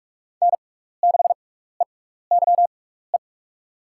Morse code, heard as a keyed tone, about 44 wpm, spaced as for 15 wpm, 690 Hz. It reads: N6EYE
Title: USA Call Signs - 5 Characters - 6x Character Spacing - 45wpm